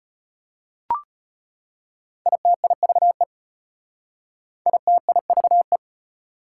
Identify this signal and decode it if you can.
Morse code, heard as a keyed tone, about 33 wpm, 700 Hz. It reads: ITIVE ITIVE